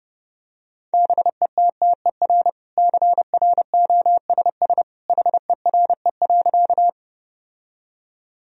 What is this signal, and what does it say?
Morse code, heard as a keyed tone, about 32 wpm, 700 Hz. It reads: BETTER CROSS HERE.
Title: Sentences from Top 1000 Words - Review Part 1 - 30wpm